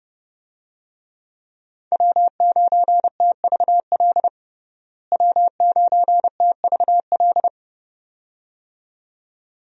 Morse code, heard as a keyed tone, about 30 wpm, 700 Hz. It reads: W9TVL W9TVL
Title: USA Call Signs - 5 Characters - Repeated 2x - 30wpm